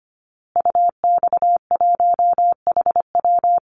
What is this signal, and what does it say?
Morse code, heard as a keyed tone, about 25 wpm, 700 Hz. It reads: UX1HW
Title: Call Signs Repeated 2x - 25wpm